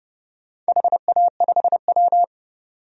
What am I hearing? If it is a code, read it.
Morse code, heard as a keyed tone, about 31 wpm, 700 Hz. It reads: HA5W
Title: DXCC-100 Call Signs - 4 Characters - 30wpm